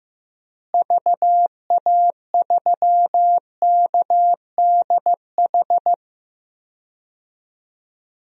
Morse code, heard as a keyed tone, about 15 wpm, 700 Hz. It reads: VA3KDH
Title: DXCC-100 Call Signs - 6 Characters - Repeated 2x - 15wpm